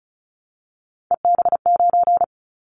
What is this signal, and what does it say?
Morse code, heard as a keyed tone, about 34 wpm, 700 Hz. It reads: EB9
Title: Callsign Prefixes for Top 100 DXCC - 35wpm